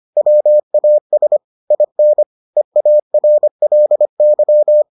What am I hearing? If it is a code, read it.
Morse code, heard as a keyed tone, about 25 wpm, 600 Hz. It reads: WAS IN EARLY